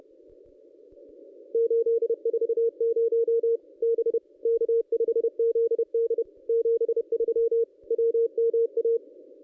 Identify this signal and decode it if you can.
Morse code, heard as a keyed tone, about 28 wpm, 440 Hz. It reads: 840 B K5ZD 73 WMA
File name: contests / ss-cw-k5tr-op-n5rz -1200